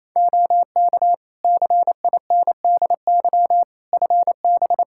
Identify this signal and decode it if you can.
Morse code, heard as a keyed tone, about 28 wpm, 700 Hz. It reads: OK CINDY FB
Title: K1USN SST QSO Practice 28wpm